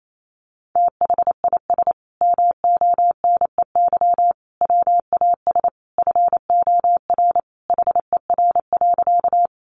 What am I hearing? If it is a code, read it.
Morse code, heard as a keyed tone, about 28 wpm, 700 Hz. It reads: THIS MONEY WAS FOR HER.
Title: Sentences from Top 400 Words 28wpm